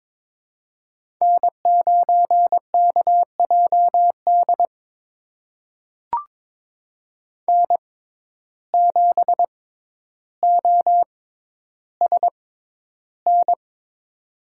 Morse code, heard as a keyed tone, about 22 wpm, spaced as for 11 wpm, 700 Hz. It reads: N9KJD N7OSN
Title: USA Call Signs - 5 Characters - 6x Character Spacing - 22wpm